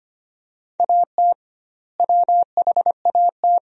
Morse code, heard as a keyed tone, about 25 wpm, 700 Hz. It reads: AT WHAT
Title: 2x Word-Spacing - Sentences from Top 100 Words - 25wpm